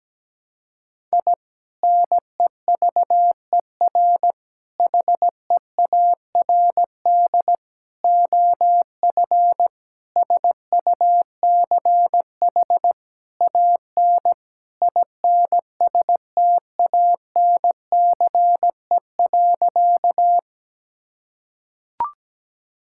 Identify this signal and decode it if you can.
Morse code, heard as a keyed tone, about 17 wpm, 700 Hz. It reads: I NEVER HEARD OF SUCH AN INSTANCE.
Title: Sentences from Top 1000 Words - Review Part 5 - 17wpm